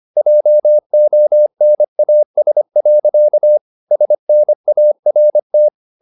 Morse code, heard as a keyed tone, about 25 wpm, 600 Hz. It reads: JONAS. SNART
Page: Ferie 25 wpm